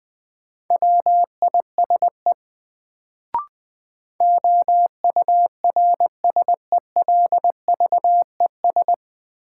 Morse code, heard as a keyed tone, about 20 wpm, 700 Hz. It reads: WISE OURSELVES
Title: Sentences from Top 1600 Words - New Word Drill - 20wpm